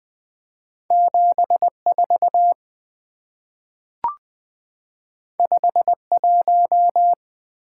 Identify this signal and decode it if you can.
Morse code, heard as a keyed tone, about 20 wpm, 700 Hz. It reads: 74 51
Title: Two Letter-Number Combinations - 20wpm